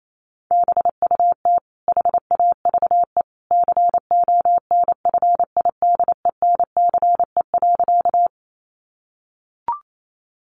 Morse code, heard as a keyed tone, about 28 wpm, 700 Hz. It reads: BUT HAVE CONFIDENCE.
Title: Sentences from Top 1500 Words 28wpm